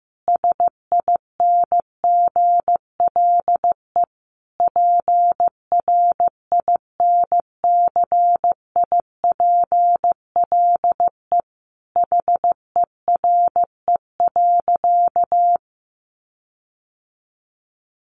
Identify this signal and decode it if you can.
Morse code, heard as a keyed tone, about 15 wpm, 700 Hz. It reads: SINGLE PRINCIPLE HERE.
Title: Sentences from Top 1400 Words 15wpm